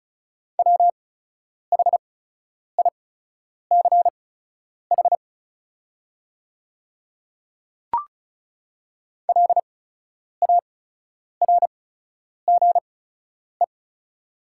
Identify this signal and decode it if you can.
Morse code, heard as a keyed tone, about 36 wpm, spaced as for 10 wpm, 690 Hz. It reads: WHICH LARGE
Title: 8x Character Spacing - Top 200 Words - 35wpm